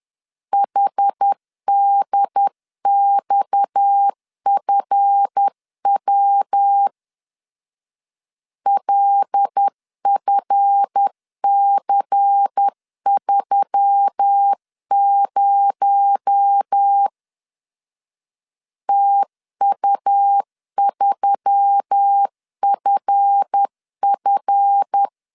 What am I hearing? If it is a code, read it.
Morse code, heard as a keyed tone, about 10 wpm, 800 Hz. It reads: HDXFW LFC30 TU3FF
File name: LEZIONI / 7 GF9Z / 7 GF 10wpm 2514